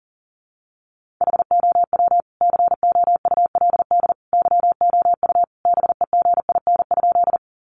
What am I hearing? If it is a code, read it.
Morse code, heard as a keyed tone, about 40 wpm, 700 Hz. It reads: HOW COULD YOU BEGIN?
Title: Sentences from Top 300 Words 40wpm